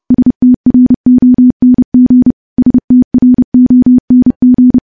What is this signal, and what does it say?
Morse code, heard as a keyed tone, about 31 wpm, 250 Hz. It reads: STRONG STRONG
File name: morse / morse-training-30WPM